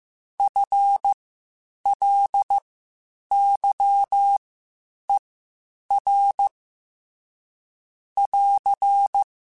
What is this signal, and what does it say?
Morse code, heard as a keyed tone, about 15 wpm, spaced as for 8 wpm, 800 Hz. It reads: FLYER <AR>